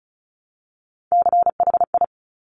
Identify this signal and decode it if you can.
Morse code, heard as a keyed tone, about 36 wpm, 700 Hz. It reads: CHI
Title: Top Three Letter Combinations in Words 35wpm